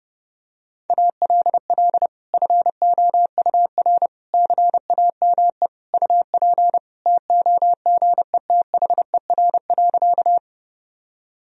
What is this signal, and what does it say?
Morse code, heard as a keyed tone, about 30 wpm, 700 Hz. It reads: ALL FOUR CAME UP TOGETHER.
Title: Sentences from Top 1000 Words - Review Part 1 - 30wpm